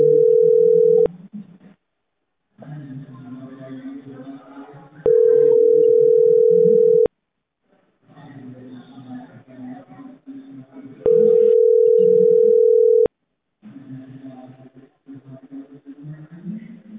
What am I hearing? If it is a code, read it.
Ringback tone (North America), 440 + 480 Hz.